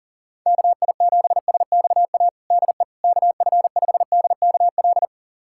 Morse code, heard as a keyed tone, about 40 wpm, 700 Hz. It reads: KI7SXA DE KF5DKL
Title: USA Callsign DE Callsign - 6 Characters - 40wpm